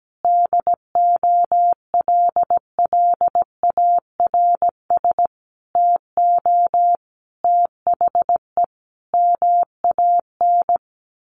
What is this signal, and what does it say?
Morse code, heard as a keyed tone, about 17 wpm, 700 Hz. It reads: DOLLARS TO THE MAN
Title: A Sound Of Thunder 17wpm